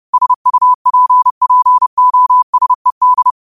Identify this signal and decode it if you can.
Morse code, heard as a keyed tone, about 30 wpm, 1000 Hz. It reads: SUPPOSED
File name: qotdfiles / 30wpm